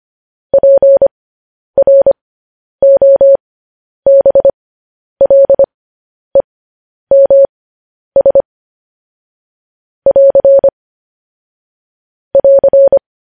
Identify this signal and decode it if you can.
Morse code, heard as a keyed tone, about 25 wpm, spaced as for 10 wpm, 550 Hz. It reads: PROBLEMS <AR> <AR>
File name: podcast / News headlines-1x-25x10